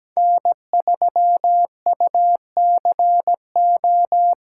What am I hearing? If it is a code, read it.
Morse code, heard as a keyed tone, about 17 wpm, 700 Hz. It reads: N3UCO
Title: USA Callsign DE Callsign - 5 Characters - 17wpm